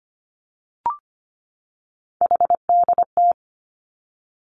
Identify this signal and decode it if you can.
Morse code, heard as a keyed tone, about 25 wpm, 700 Hz. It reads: HDT
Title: Three Challenging Letter-Number Combinations 25wpm